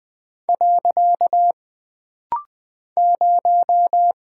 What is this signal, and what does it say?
Morse code, heard as a keyed tone, about 20 wpm, 700 Hz. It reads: . 0
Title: Single Character (Letter:Number:Punctuation:Prosign) - Rapid-Fire - 20wpm